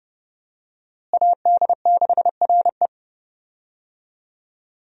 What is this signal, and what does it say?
Morse code, heard as a keyed tone, about 30 wpm, 700 Hz. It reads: AD6RE
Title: USA Call Signs - 5 Characters - Repeated 3x - 30wpm